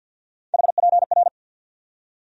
Morse code, heard as a keyed tone, about 47 wpm, 700 Hz. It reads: HPR
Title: USA Call Sign Postfixes - 3 Letters 50wpm